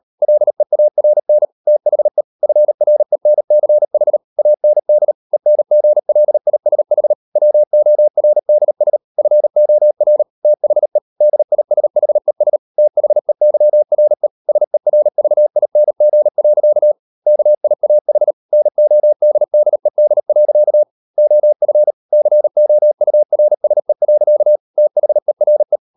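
Morse code, heard as a keyed tone, about 38 wpm, 600 Hz. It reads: LEARN THE FRENCH AND ENGLISH WORDS FOR THE DISHES THEYRE SERVING. KIAH NODDED. OF COURSE. THERE
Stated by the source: Morse code practice at 38 WPM, sending Ken Follett's novel Never in English, from CW med Gnister.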